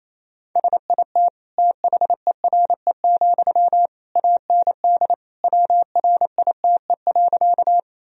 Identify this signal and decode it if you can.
Morse code, heard as a keyed tone, about 28 wpm, 700 Hz. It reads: SIT THERE, AND WRITE.
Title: Sentences from Top 1000 Words - Review Part 1 - 28wpm